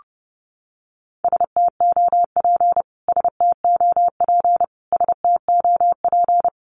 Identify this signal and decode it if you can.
Morse code, heard as a keyed tone, about 30 wpm, 700 Hz. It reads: STOP STOP STOP
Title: Top 300 Words Repeated 30wpm